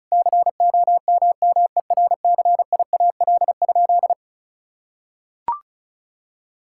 Morse code, heard as a keyed tone, about 34 wpm, 700 Hz. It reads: COMMERCIAL?
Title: Sentences from Top 1000 Words - Review Part 5 - 35wpm